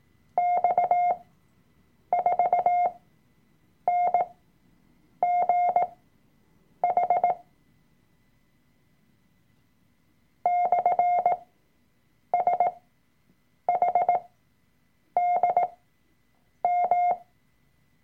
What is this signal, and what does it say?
Morse code, heard as a keyed tone, about 18 wpm, spaced as for 7 wpm, 710 Hz. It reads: X4NGH /SHDM
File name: Workshop Dateien / CW-Online-Kurs / Lektion 003